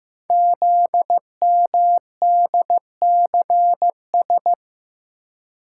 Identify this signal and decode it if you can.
Morse code, heard as a keyed tone, about 15 wpm, 700 Hz. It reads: ZMDCS
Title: Code Groups - 5 Letters - 15wpm